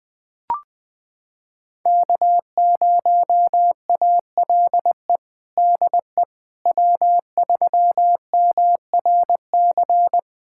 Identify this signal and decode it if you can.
Morse code, heard as a keyed tone, about 20 wpm, 700 Hz. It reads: K0ALE DE W3MRC
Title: USA Callsign DE Callsign - 5 Characters - 20wpm